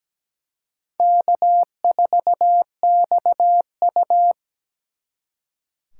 Morse code, heard as a keyed tone, about 20 wpm, 700 Hz. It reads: K4XU
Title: Speed Racing - USA 1x2 Call Signs 17wpm